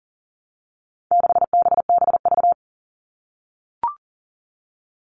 Morse code, heard as a keyed tone, about 40 wpm, 700 Hz. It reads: 6BBV
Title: All Cut Numbers - 4-Digit 40wpm